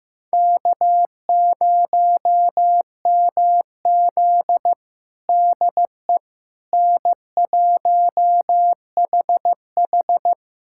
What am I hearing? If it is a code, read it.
Morse code, heard as a keyed tone, about 15 wpm, 700 Hz. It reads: K0MZ DE N1HH
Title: USA Callsign DE Callsign - 4 Characters - 15wpm